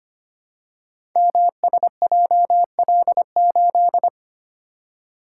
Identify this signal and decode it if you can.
Morse code, heard as a keyed tone, about 25 wpm, 700 Hz. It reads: MSJL8
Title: Code Groups - 5 Letter:Numbers - 25wpm